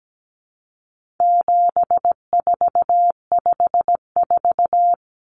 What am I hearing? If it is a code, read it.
Morse code, heard as a keyed tone, about 17 wpm, 700 Hz. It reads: 7454
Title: Numbers - 4-Digit 17wpm